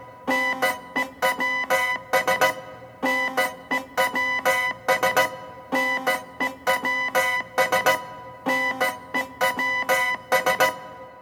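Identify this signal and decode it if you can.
Morse code, heard as a keyed tone, about 16 wpm, 2000 Hz, with noise. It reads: NEWS NEWS NEWS NEWS
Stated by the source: Morse code used as music.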